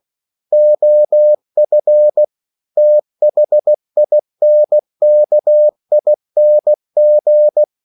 Morse code, heard as a keyed tone, about 16 wpm, 600 Hz. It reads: OF THINKING
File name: CW / Practice / EnglishQuotations0216